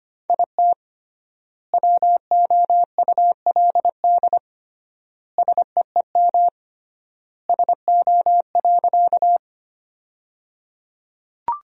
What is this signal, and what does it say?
Morse code, heard as a keyed tone, about 25 wpm, 700 Hz. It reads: IT WOULD SEEM SO.
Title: 3x Word-Spacing - Sentences from Top 300 Words - 25wpm